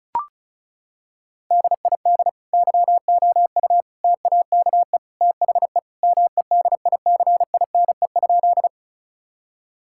Morse code, heard as a keyed tone, about 36 wpm, 700 Hz. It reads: DID YOU TAKE THE MEDICINE?
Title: Sentences from Top 1000 Words - Review Part 6 - 35wpm